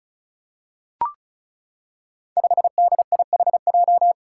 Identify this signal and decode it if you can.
Morse code, heard as a keyed tone, about 34 wpm, 700 Hz. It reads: 5DIHJ
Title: Five Challenging Letter-Number Combinations 35wpm